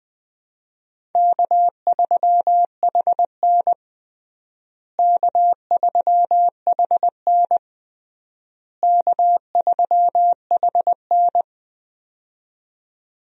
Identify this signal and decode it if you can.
Morse code, heard as a keyed tone, about 21 wpm, 700 Hz. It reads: K3HN K3HN K3HN
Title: USA Call Signs - 4 Characters - Repeated 3x - 20wpm